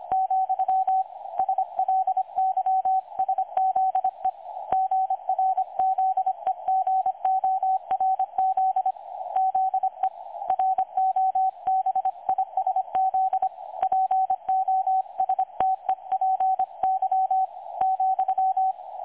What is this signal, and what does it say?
Morse code, heard as a keyed tone, about 25 wpm, 760 Hz. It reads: , SLYSZE GRZEGORZ ZE ROBISZ POSTEPY ,
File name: dlazaawansowanych / nagraniazpasma / websdr recording 3565.0kHz